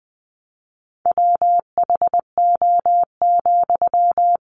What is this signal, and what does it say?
Morse code, heard as a keyed tone, about 20 wpm, 700 Hz. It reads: WHO,
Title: Sentences from Top 100 Words 20wpm